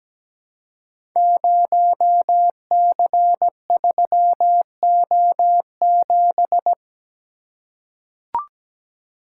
Morse code, heard as a keyed tone, about 17 wpm, 700 Hz. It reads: 0C3O7
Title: Code Groups - 5 Letter:Numbers - 17wpm